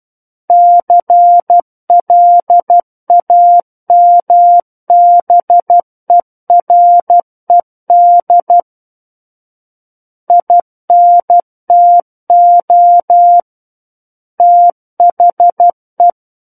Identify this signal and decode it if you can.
Morse code, heard as a keyed tone, about 12 wpm, 700 Hz. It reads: CLAMBERED INTO THE
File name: Book One / 03-Chapter 3 - 12 WPM